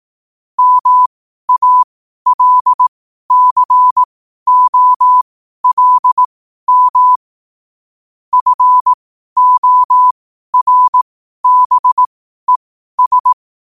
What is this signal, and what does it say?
Morse code, heard as a keyed tone, about 18 wpm, spaced as for 12 wpm, 1000 Hz. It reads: MALCOLM FORBES
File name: qotdfiles / 12wpm